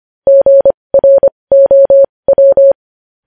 Morse code, heard as a keyed tone, about 25 wpm, spaced as for 20 wpm, 550 Hz. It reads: GROW
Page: News Headlines in Morse Code at 20 WPM